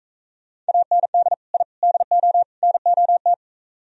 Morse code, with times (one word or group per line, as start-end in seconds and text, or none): 0.7-1.3 AND
1.5-1.6 I
1.8-2.4 DO
2.6-3.3 NOT